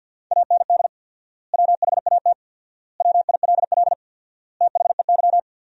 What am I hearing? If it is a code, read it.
Morse code, heard as a keyed tone, about 50 wpm, 700 Hz. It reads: AND WHAT WILL THEY
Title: 4x Word-Spacing - Sentences from Top 200 Words - 50wpm